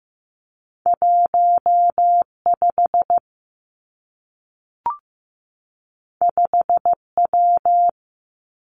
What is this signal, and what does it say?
Morse code, heard as a keyed tone, about 15 wpm, 700 Hz. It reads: 1 5 5 W
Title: Two Challenging Letter-Number Combinations 15wpm